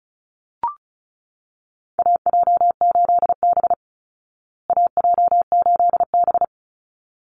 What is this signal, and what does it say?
Morse code, heard as a keyed tone, about 34 wpm, 700 Hz. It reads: AJ8B AJ8B
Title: Call Signs Repeated 3x - 35wpm